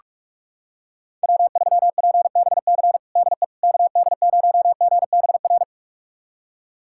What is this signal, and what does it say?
Morse code, heard as a keyed tone, about 42 wpm, 700 Hz. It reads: W3PBC DE KD0GBR
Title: USA Callsign DE Callsign 45wpm